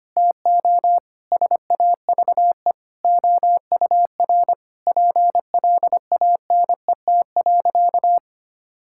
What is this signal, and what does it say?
Morse code, heard as a keyed tone, about 25 wpm, 700 Hz. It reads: TO SAVE OUR PLANET.
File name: practice / 25wpm / Difficult Sentences with Hints - 25wpm